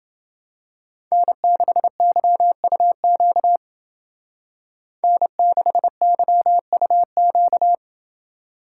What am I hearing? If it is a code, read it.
Morse code, heard as a keyed tone, about 28 wpm, 700 Hz. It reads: N6YUQ N6YUQ